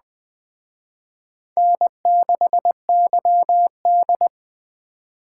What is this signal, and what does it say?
Morse code, heard as a keyed tone, about 21 wpm, 700 Hz. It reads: N6YD